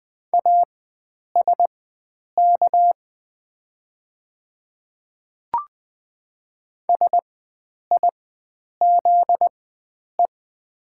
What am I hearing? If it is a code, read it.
Morse code, heard as a keyed tone, about 20 wpm, spaced as for 9 wpm, 700 Hz. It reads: ASK SIZE
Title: 4x Character Spacing - Top 500 Words - 20wpm